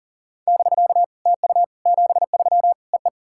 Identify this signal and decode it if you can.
Morse code, heard as a keyed tone, about 37 wpm, 700 Hz. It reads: <BK> TU 73 EE
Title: SOTA QSO Practice 40wpm